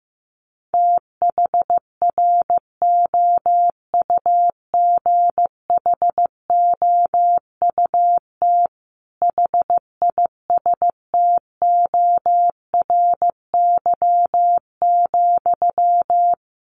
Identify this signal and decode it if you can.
Morse code, heard as a keyed tone, about 14 wpm, 700 Hz. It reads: THROUGHOUT HISTORY,
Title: Sentences from Top 800 Words 15wpm